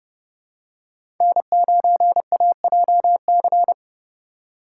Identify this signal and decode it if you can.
Morse code, heard as a keyed tone, about 30 wpm, 700 Hz. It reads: N9AJC
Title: USA Call Signs - 2x Character Spacing - 30wpm